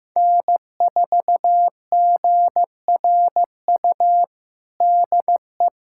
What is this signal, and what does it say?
Morse code, heard as a keyed tone, about 15 wpm, 700 Hz. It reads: N4GRU DE
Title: USA Callsign DE Callsign - 5 Characters - 15wpm